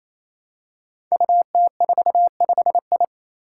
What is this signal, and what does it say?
Morse code, heard as a keyed tone, about 27 wpm, 700 Hz. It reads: UT45I
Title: Five Challenging Letter-Number Combinations 28wpm